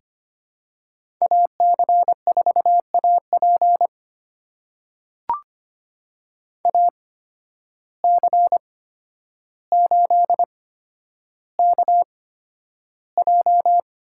Morse code, heard as a keyed tone, about 25 wpm, spaced as for 11 wpm, 700 Hz. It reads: AC4AP AC8KJ
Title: USA Call Signs - 5 Characters - 8x Character Spacing - 25wpm